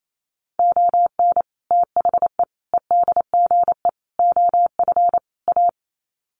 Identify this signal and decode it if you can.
Morse code, heard as a keyed tone, about 28 wpm, 700 Hz. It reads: ON THE EDGE OF A
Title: Sets of 5 Words Encore2 28wpm